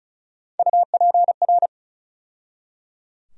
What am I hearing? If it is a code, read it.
Morse code, heard as a keyed tone, about 34 wpm, 700 Hz. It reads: UPR
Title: USA Call Sign Postfixes - 3 Letters 35wpm